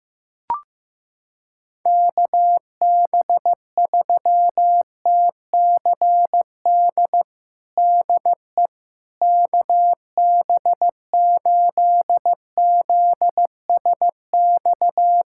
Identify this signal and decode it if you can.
Morse code, heard as a keyed tone, about 16 wpm, 700 Hz. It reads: KB3TCD DE KB8ZSX
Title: USA Callsign DE Callsign - 6 Characters - 15wpm